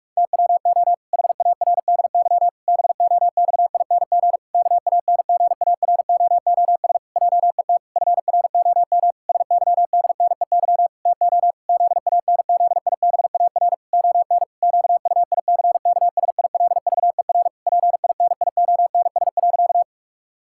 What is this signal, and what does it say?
Morse code, heard as a keyed tone, about 46 wpm, 700 Hz. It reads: TWO HARDY BOXING KANGAROOS JET FROM SYDNEY TO ZANZIBAR ON QUICKSILVER PINIONS.
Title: Pangram Challenge v2 - 45wpm